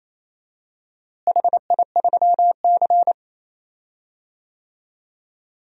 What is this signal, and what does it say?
Morse code, heard as a keyed tone, about 27 wpm, 700 Hz. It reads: HI3C
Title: DXCC-100 Call Signs - 4 Characters - Repeated 2x - 28wpm